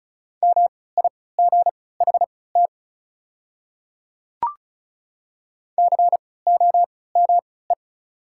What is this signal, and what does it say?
Morse code, heard as a keyed tone, about 36 wpm, spaced as for 20 wpm, 690 Hz. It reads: MIGHT COME
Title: 3x Character Spacing - Top 500 Words - 35wpm